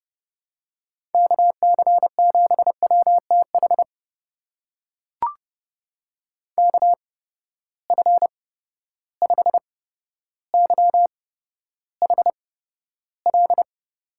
Morse code, heard as a keyed tone, about 31 wpm, spaced as for 13 wpm, 700 Hz. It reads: KC7WTH KF5YHL
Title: USA Call Signs - 6 Characters - 8x Character Spacing - 30wpm